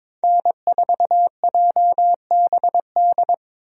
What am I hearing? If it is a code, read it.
Morse code, heard as a keyed tone, about 22 wpm, 700 Hz. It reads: N4JBD
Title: USA Call Signs - 5 Characters - 22wpm